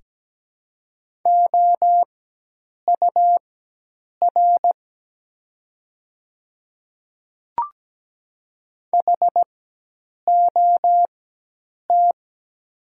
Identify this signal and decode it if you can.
Morse code, heard as a keyed tone, about 17 wpm, spaced as for 8 wpm, 700 Hz. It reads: OUR HOT
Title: 4x Character Spacing - Top 300 Words - 17wpm